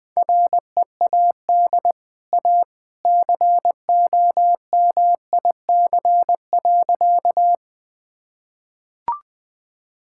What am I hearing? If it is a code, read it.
Morse code, heard as a keyed tone, about 20 wpm, 700 Hz. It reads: READ A COMIC.
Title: Easy Sentences with Hints - 20wpm